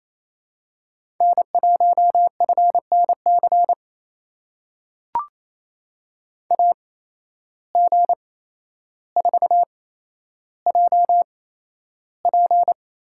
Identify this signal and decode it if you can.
Morse code, heard as a keyed tone, about 28 wpm, spaced as for 12 wpm, 700 Hz. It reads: N1FNC AG4JP
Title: USA Call Signs - 5 Characters - 8x Character Spacing - 28wpm